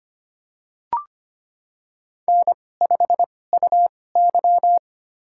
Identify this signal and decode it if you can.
Morse code, heard as a keyed tone, about 25 wpm, spaced as for 18 wpm, 700 Hz. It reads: N5UY
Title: USA Call Signs - 4 Characters - 2x Character Spacing - 25wpm